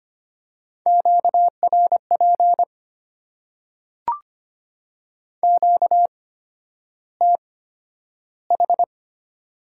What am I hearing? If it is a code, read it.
Morse code, heard as a keyed tone, about 25 wpm, spaced as for 11 wpm, 690 Hz. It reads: QRP QTH
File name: Top 90 QSO Elements - 8x Character Spacing - 25wpm